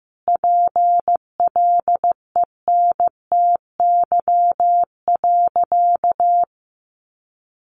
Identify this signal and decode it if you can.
Morse code, heard as a keyed tone, about 15 wpm, 700 Hz. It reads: PLENTY.
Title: Sentences from Top 1300 Words 15wpm